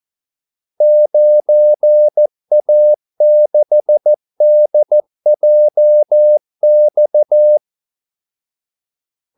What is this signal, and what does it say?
Morse code, heard as a keyed tone, about 14 wpm, 600 Hz. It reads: 9A6DJX
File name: Callsigns-14wpm